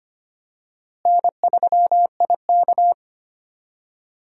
Morse code, heard as a keyed tone, about 25 wpm, 700 Hz. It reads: N3IK
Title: USA Call Signs - 4 Characters - Repeated 3x - 25wpm